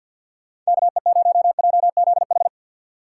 Morse code, heard as a keyed tone, about 64 wpm, 700 Hz. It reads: KE0JZH